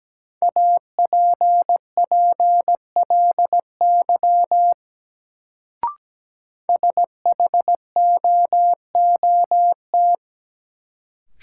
Morse code, heard as a keyed tone, about 17 wpm, 700 Hz. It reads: APPLY SHOOT
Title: Sentences from Top 1200 Words - New Word Drill - 17wpm